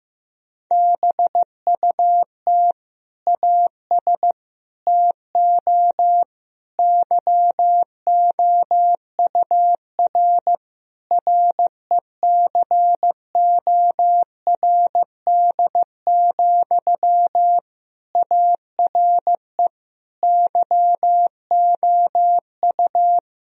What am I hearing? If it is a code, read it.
Morse code, heard as a keyed tone, about 16 wpm, 700 Hz. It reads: BUT AS TO YOUR RECORD, ARE YOU
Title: Sentences from Top 1000 Words - Review Part 3 - 15wpm